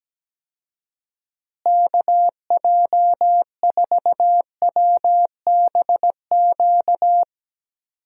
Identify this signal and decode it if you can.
Morse code, heard as a keyed tone, about 17 wpm, 700 Hz. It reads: KJ4WBQ
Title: USA Call Signs - 6 Characters - Repeated 2x - 17wpm